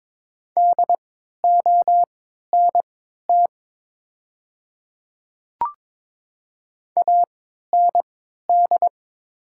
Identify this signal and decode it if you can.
Morse code, heard as a keyed tone, about 22 wpm, spaced as for 13 wpm, 700 Hz. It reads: DONT AND
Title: 3x Character Spacing - Top 300 Words - 22wpm